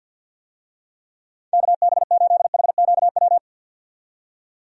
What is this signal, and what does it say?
Morse code, heard as a keyed tone, about 52 wpm, 700 Hz. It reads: KB8HQW
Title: USA Call Signs - 6 Characters - 50wpm